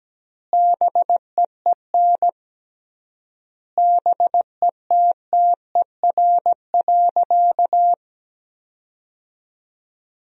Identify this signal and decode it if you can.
Morse code, heard as a keyed tone, about 17 wpm, 700 Hz. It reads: BEEN BETTER.
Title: 3x Word-Spacing - Sentences from Top 500 Words - 17wpm